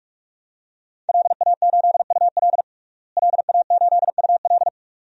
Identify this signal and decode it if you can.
Morse code, heard as a keyed tone, about 46 wpm, 700 Hz. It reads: LA8UL LA8UL
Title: DXCC-100 Call Signs - 5 Characters - Repeated 2x - 45wpm